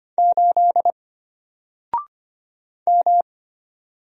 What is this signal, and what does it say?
Morse code, heard as a keyed tone, about 25 wpm, 700 Hz. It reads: 8 M
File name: Single-Letter-Number-25wpm